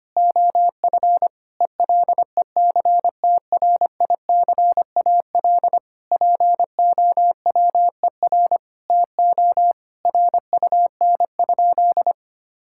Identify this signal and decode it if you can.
Morse code, heard as a keyed tone, about 25 wpm, 700 Hz. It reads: OF ELECTRICAL POWER TO RUN?
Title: Sentences from Top 1000 Words - Review Part 6 - 25wpm